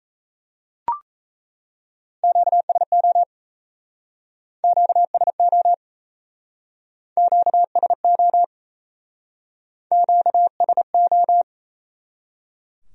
Morse code, heard as a keyed tone, about 36 wpm, 700 Hz. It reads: QSO QSO QSO QSO